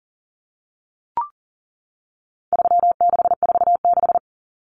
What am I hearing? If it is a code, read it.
Morse code, heard as a keyed tone, about 40 wpm, 700 Hz. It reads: 3646
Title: Common Cut Numbers - 4-Digit 40wpm